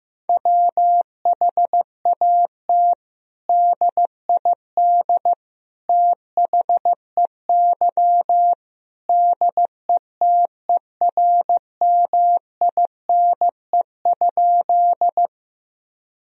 Morse code, heard as a keyed tone, about 15 wpm, 700 Hz. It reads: WHAT DID THEY DETERMINE?
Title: Sentences from Top 1000 Words - Review Part 4 - 15wpm